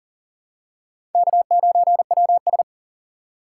Morse code, heard as a keyed tone, about 40 wpm, 700 Hz. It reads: K9WS
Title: USA Call Signs - 4 Characters - 3x Character Spacing - 40wpm